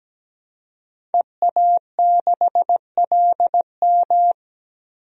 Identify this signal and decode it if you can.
Morse code, heard as a keyed tone, about 17 wpm, 700 Hz. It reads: EA6LM
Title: DXCC-100 Call Signs - 5 Characters - Repeated 3x - 17wpm